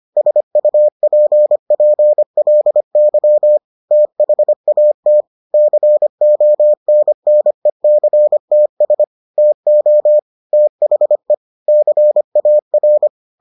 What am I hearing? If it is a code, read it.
Morse code, heard as a keyed tone, about 25 wpm, 600 Hz. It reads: SUPPLY THAT CONNECTS TO THE CAR